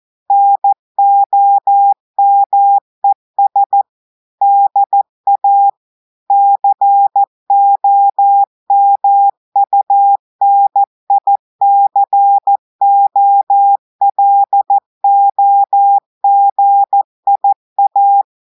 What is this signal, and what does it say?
Morse code, heard as a keyed tone, about 14 wpm, 800 Hz. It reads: NOMES DA COMUNICOLOGIA
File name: audios / portugais / portugais texte en clair 1 vitesse 14 mots minutes